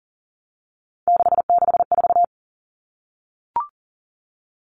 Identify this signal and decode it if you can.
Morse code, heard as a keyed tone, about 40 wpm, 700 Hz. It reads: 664
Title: Common Cut Numbers - 3-Digit 40wpm